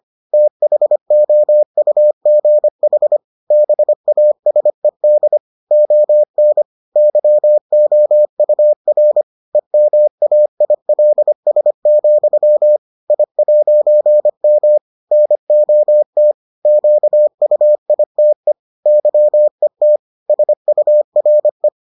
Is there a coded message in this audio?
Yes: Morse code from 0.3 to 21.7 s, heard as a keyed tone, about 25 wpm, 600 Hz. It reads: THOUGH BASED ON YOUR EMAILS, I'M NOT QUITE YET SURE